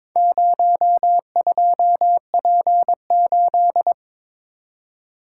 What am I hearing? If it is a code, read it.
Morse code, heard as a keyed tone, about 23 wpm, 700 Hz. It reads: 02P8
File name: practice / 22wpm / Four Letter-Number Combinations - 22wpm